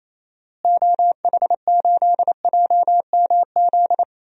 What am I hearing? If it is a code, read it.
Morse code, heard as a keyed tone, about 28 wpm, 700 Hz. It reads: OH8JMZ
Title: DXCC-100 Call Signs - 6 Characters - Repeated 2x - 28wpm